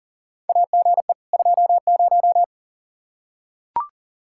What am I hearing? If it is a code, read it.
Morse code, heard as a keyed tone, about 40 wpm, 700 Hz. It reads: AGE 20
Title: Practice QSO - Exchange 3 - More Info - 40wpm